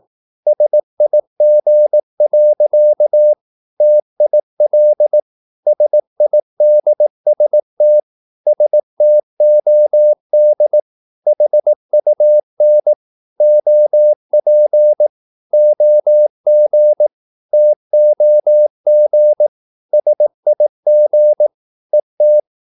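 Morse code, heard as a keyed tone, about 18 wpm, 600 Hz. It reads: SIG. TIL SIDST STOD HUN OP OG TOG SIG ET
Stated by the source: Slow (QRS) Morse practice at 18 wpm, CW med Gnister.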